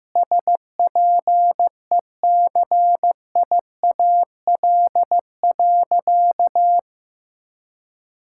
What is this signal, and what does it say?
Morse code, heard as a keyed tone, about 15 wpm, 700 Hz. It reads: SPECIAL.
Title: Sentences from Top 1600 Words 15wpm